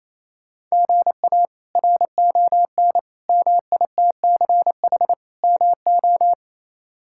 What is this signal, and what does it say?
Morse code, heard as a keyed tone, about 28 wpm, 700 Hz. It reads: GA RON MITCH MO
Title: K1USN SST QSO Practice - 28wpm